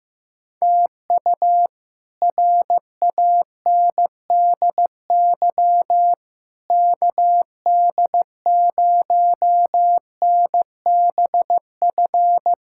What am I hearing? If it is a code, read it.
Morse code, heard as a keyed tone, about 16 wpm, 700 Hz. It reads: TU RANDY KD0NBF
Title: K1USN SST QSO Practice - 15wpm